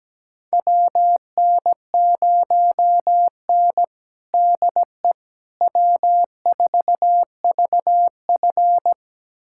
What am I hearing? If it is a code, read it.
Morse code, heard as a keyed tone, about 17 wpm, 700 Hz. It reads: WN0N DE W4VF
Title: USA Callsign DE Callsign - 4 Characters - 17wpm